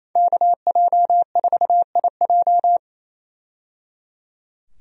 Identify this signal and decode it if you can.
Morse code, heard as a keyed tone, about 27 wpm, 700 Hz. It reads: KJ4IJ
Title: USA Call Signs - 5 Characters - 28wpm